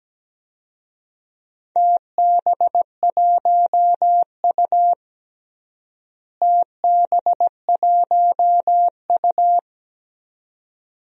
Morse code, heard as a keyed tone, about 17 wpm, 700 Hz. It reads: TB1U TB1U
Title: DXCC-100 Call Signs - 4 Characters - Repeated 2x - 17wpm